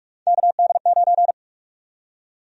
Morse code, heard as a keyed tone, about 42 wpm, 700 Hz. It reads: KD9
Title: USA Call Sign Prefixes - Repeated 2x - 45wpm